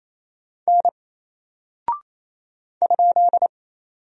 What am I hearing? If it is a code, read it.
Morse code, heard as a keyed tone, about 28 wpm, 700 Hz. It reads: N ?
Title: Single Character (Letter:Number:Punctuation:Prosign) Alt 28wpm